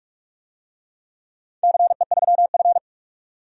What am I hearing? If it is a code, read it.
Morse code, heard as a keyed tone, about 44 wpm, 700 Hz. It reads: CE3F
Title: DXCC-100 Call Signs - 4 Characters - 45wpm